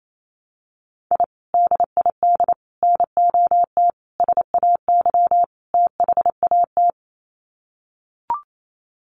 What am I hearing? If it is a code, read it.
Morse code, heard as a keyed tone, about 28 wpm, 700 Hz. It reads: I DID NOT SAY THAT
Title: Sets of 5 Words Encore2 28wpm